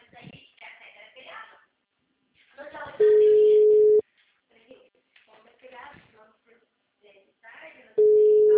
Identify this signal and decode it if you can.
Ringback tone (Europe, 425 Hz), with speech.
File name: out- -2024- - - .5